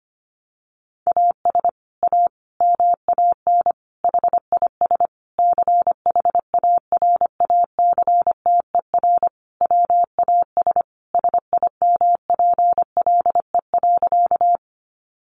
Morse code, heard as a keyed tone, about 25 wpm, 700 Hz. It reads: AS A MAN HIS CHARACTER WAS SIMPLE.
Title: Sentences from Top 1200 Words 25wpm